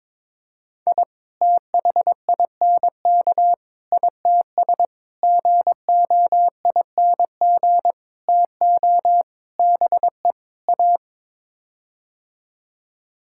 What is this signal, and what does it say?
Morse code, heard as a keyed tone, about 22 wpm, 700 Hz. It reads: I THINK ITS GOING TO BE A
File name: practice / 22wpm / Sets of 7 Words - 22wpm